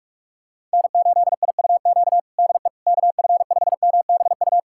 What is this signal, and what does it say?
Morse code, heard as a keyed tone, about 46 wpm, 700 Hz. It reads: N8IUQ DE KF5MBU
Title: USA Callsign DE Callsign 45wpm